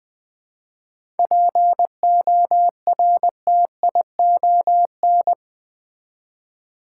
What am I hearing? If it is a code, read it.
Morse code, heard as a keyed tone, about 20 wpm, 700 Hz. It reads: PORTION
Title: 7 Letter Words - First 3 Letters Given - 20wpm